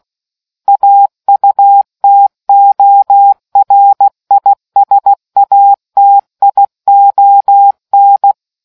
Morse code, heard as a keyed tone, about 16 wpm, 800 Hz. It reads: AUTORISATION